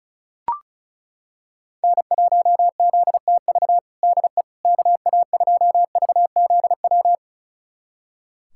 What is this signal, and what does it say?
Morse code, heard as a keyed tone, about 34 wpm, 700 Hz. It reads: N1ZTV DE KA2VZW
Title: USA Callsign DE Callsign 35wpm